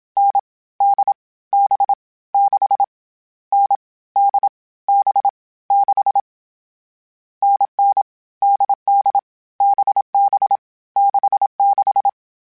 Morse code, heard as a keyed tone, about 27 wpm, 800 Hz. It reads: N D B 6 N D B 6 NN DD BB 66
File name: Variable Speed / Variable Speed-mp3s / GZ GZ7 sp @25